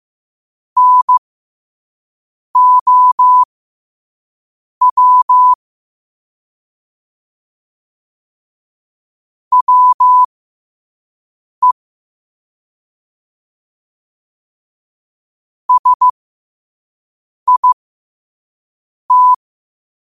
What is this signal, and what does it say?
Morse code, heard as a keyed tone, about 15 wpm, spaced as for 5 wpm, 1000 Hz. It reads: NOW WE SIT
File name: 5wpm